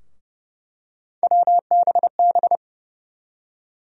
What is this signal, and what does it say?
Morse code, heard as a keyed tone, about 29 wpm, 700 Hz. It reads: WBB
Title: USA Call Sign Postfixes - 3 Letters 30wpm